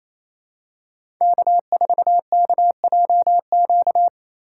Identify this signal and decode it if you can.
Morse code, heard as a keyed tone, about 28 wpm, 700 Hz. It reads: K4KJQ
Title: USA Call Signs - 3x Character Spacing - 28wpm